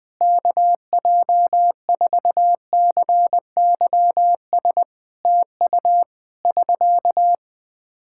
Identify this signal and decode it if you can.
Morse code, heard as a keyed tone, about 20 wpm, 700 Hz. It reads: KJ4CYS TU <SK>
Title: K1USN SST QSO Practice 20wpm